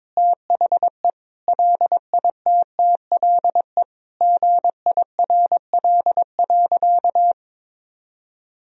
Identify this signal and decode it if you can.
Morse code, heard as a keyed tone, about 22 wpm, 700 Hz. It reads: THE LITTLE GIRL.
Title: Sentences from Top 400 Words and Abbreviations 22wpm